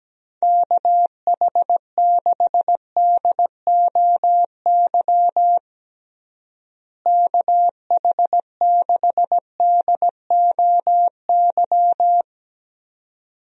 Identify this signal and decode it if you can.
Morse code, heard as a keyed tone, about 17 wpm, 700 Hz. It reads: KH6DOY KH6DOY
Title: DXCC-100 Call Signs - 6 Characters - Repeated 2x - 17wpm